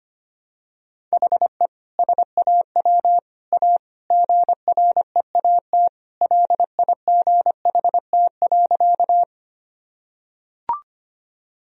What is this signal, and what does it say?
Morse code, heard as a keyed tone, about 25 wpm, 700 Hz. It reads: HE SAW A GREAT LIGHT.
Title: Sentences from Top 1000 Words - Review Part 1 - 25wpm